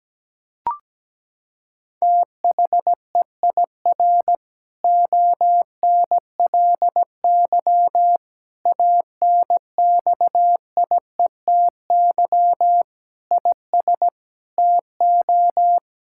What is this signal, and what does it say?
Morse code, heard as a keyed tone, about 17 wpm, 700 Hz. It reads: THEIR ONLY ANXIETY IS TO
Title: Sentences from Top 1600 Words 17wpm